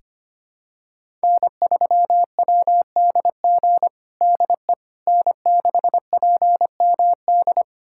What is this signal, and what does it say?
Morse code, heard as a keyed tone, about 25 wpm, 700 Hz. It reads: N3WDG DE N6PMD
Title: USA Callsign DE Callsign - 5 Characters - 25wpm